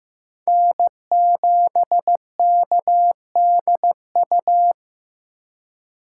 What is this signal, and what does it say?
Morse code, heard as a keyed tone, about 15 wpm, 700 Hz. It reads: N7KDU
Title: USA Call Signs - 5 Characters - 2x Character Spacing - 15wpm